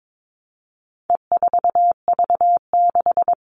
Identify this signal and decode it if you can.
Morse code, heard as a keyed tone, about 22 wpm, 700 Hz. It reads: E4V6
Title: All Cut Numbers - 4-Digit 22wpm